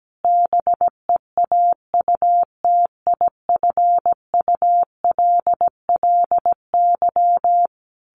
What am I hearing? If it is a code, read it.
Morse code, heard as a keyed tone, about 17 wpm, 700 Hz. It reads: BEAUTIFULLY
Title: A Sound Of Thunder 17wpm